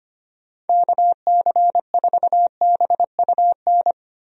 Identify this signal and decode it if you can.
Morse code, heard as a keyed tone, about 25 wpm, 700 Hz. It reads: KC4BUN
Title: USA Call Signs - 6 Characters - Repeated 2x - 25wpm